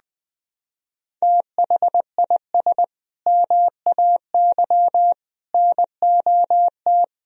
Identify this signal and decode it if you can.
Morse code, heard as a keyed tone, about 20 wpm, 700 Hz. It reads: THIS MAY NOT
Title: Sentences from Top 1000 Words - Review Part 3 - 20wpm